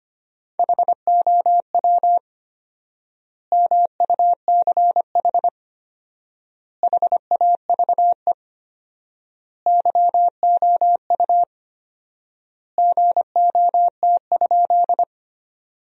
Morse code, heard as a keyed tone, about 25 wpm, 700 Hz. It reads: HOW MUCH HAVE YOU GOT?
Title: 4x Word-Spacing - Sentences from Top 300 Words - 25wpm